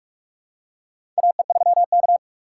Morse code, heard as a keyed tone, about 43 wpm, 700 Hz. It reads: AE3K
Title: USA Call Signs - 4 Characters - 6x Character Spacing - 45wpm